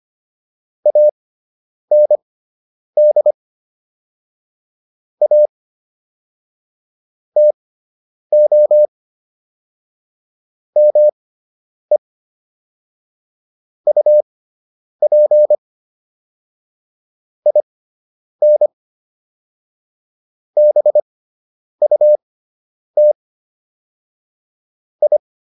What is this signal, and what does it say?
Morse code, heard as a keyed tone, about 25 wpm, spaced as for 9 wpm, 600 Hz. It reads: AND A TO ME UP IN BUT I